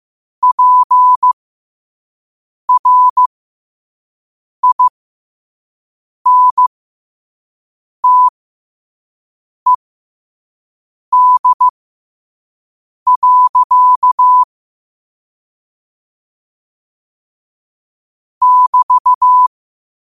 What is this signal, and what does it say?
Morse code, heard as a keyed tone, about 15 wpm, spaced as for 5 wpm, 1000 Hz. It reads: PRINTED. <BT>
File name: qotdfiles / 5wpm